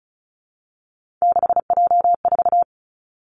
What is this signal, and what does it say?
Morse code, heard as a keyed tone, about 36 wpm, 700 Hz. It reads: 6J4
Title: Three Challenging Letter-Number Combinations 35wpm